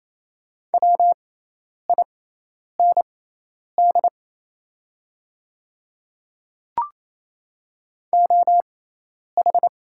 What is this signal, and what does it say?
Morse code, heard as a keyed tone, about 27 wpm, spaced as for 10 wpm, 690 Hz. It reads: WIND OH
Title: 6x Character Spacing - Top 500 Words - 28wpm